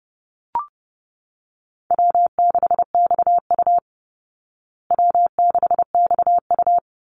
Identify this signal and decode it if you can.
Morse code, heard as a keyed tone, about 30 wpm, 700 Hz. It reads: W6XU W6XU
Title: Easy Call Signs Repeated 3x - 30wpm